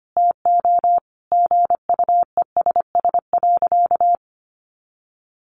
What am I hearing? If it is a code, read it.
Morse code, heard as a keyed tone, about 25 wpm, 700 Hz. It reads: TO GUESS.
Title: Sentences from Top 1500 Words 25wpm